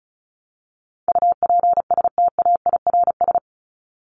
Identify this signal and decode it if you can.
Morse code, heard as a keyed tone, about 34 wpm, 700 Hz. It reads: UPSTAIRS
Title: Top 1751-2000 Words 35wpm